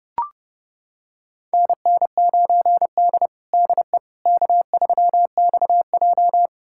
Morse code, heard as a keyed tone, about 30 wpm, 700 Hz. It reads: NN9D DE K3XJ
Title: USA Callsign DE Callsign - 4 Characters - 30wpm